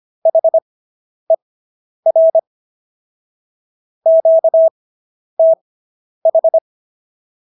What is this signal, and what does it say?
Morse code, heard as a keyed tone, about 25 wpm, spaced as for 10 wpm, 650 Hz. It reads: HER QTH